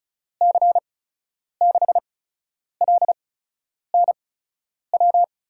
Morse code, heard as a keyed tone, about 36 wpm, spaced as for 10 wpm, 700 Hz. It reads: C6LNW
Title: USA Call Signs - 6 Characters - 8x Character Spacing - 35wpm